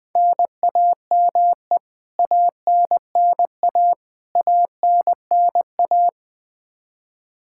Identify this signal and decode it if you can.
Morse code, heard as a keyed tone, about 20 wpm, 700 Hz. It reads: NAME ANNA ANNA
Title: Practice QSO - Exchange 1 - Beginning - 20wpm